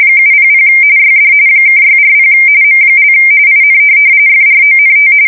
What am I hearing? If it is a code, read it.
RTTY, about 45.45 baud. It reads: 700 YD (640 M) IN FRONT
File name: fsk170 / USS-Missouri